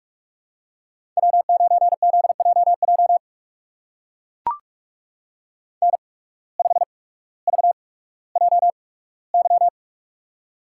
Morse code, heard as a keyed tone, about 41 wpm, spaced as for 19 wpm, 700 Hz. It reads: W9ZJJ N5VJY
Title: USA Call Signs - 5 Characters - 8x Character Spacing - 45wpm